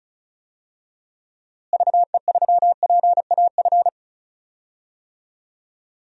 Morse code, heard as a keyed tone, about 34 wpm, 700 Hz. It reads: VE3PAF
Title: DXCC-100 Call Signs - 6 Characters - Repeated 3x - 35wpm